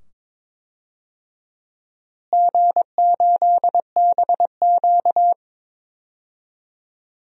Morse code, heard as a keyed tone, about 22 wpm, 700 Hz. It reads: G8BQ
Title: DXCC-100 Call Signs - 4 Characters - Repeated 2x - 22wpm